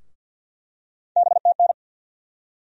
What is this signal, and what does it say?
Morse code, heard as a keyed tone, about 48 wpm, 700 Hz. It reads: BTN
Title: USA Call Sign Postfixes - 3 Letters 50wpm